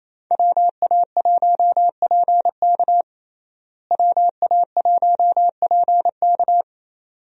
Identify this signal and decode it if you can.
Morse code, heard as a keyed tone, about 28 wpm, 700 Hz. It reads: WA1PK WA1PK
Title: USA Call Signs - 5 Characters - Repeated 2x - 28wpm